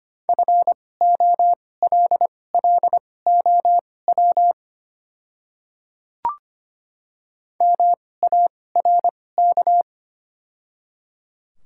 Morse code, heard as a keyed tone, about 25 wpm, spaced as for 18 wpm, 700 Hz. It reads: FOLLOW MARK
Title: 2x Character Spacing - Top 300 Words - 25wpm